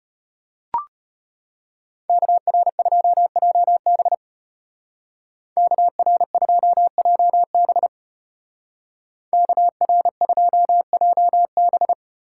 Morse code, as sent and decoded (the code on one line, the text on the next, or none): -.- .-. ..--- .--- -... / -.- .-. ..--- .--- -... / -.- .-. ..--- .--- -...
KR2JB KR2JB KR2JB